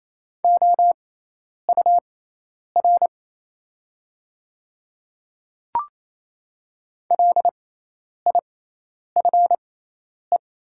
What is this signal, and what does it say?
Morse code, heard as a keyed tone, about 28 wpm, spaced as for 10 wpm, 700 Hz. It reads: OUR LIFE
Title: 6x Character Spacing - Top 300 Words - 28wpm